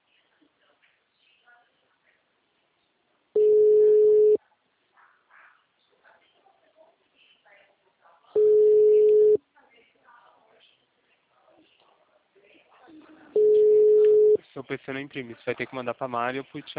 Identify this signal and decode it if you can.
Ringback tone (Europe, 425 Hz), with speech.